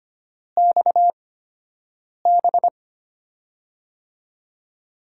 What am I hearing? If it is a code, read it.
Morse code, heard as a keyed tone, about 25 wpm, spaced as for 7 wpm, 700 Hz. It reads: XB
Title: USA Call Signs - 6 Characters - 8x Character Spacing - 25wpm